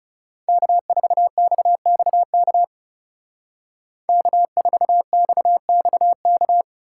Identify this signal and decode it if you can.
Morse code, heard as a keyed tone, about 39 wpm, 700 Hz. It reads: K4XXK K4XXK